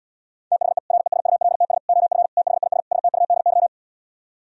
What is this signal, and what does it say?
Morse code, heard as a keyed tone, about 77 wpm, 700 Hz. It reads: THE RESULTS OF THIS STUDY